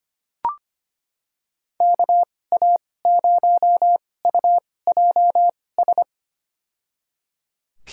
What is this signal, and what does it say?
Morse code, heard as a keyed tone, about 25 wpm, spaced as for 18 wpm, 700 Hz. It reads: KA0UJS
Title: USA Call Signs - 2x Character Spacing - 25wpm